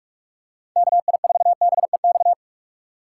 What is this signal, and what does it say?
Morse code, heard as a keyed tone, about 41 wpm, 700 Hz. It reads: KI4BEX
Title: USA Call Signs - 6 Characters - 2x Character Spacing - 45wpm